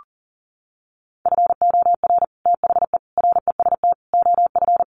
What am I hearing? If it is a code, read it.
Morse code, heard as a keyed tone, about 40 wpm, 700 Hz. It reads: FOR THE REST OF
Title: Sets of 4 Words 40wpm